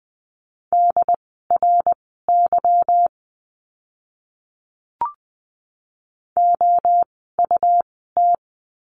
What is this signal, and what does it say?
Morse code, heard as a keyed tone, about 20 wpm, spaced as for 15 wpm, 700 Hz. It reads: DRY OUT
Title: 146 Lesson 24f - Letter y - Words 20wpm